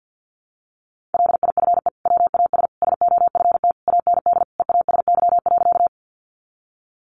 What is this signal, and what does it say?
Morse code, heard as a keyed tone, about 48 wpm, 700 Hz. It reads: LIFE WAS SOFT AND EASY.
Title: Sentences from Top 1500 Words 50wpm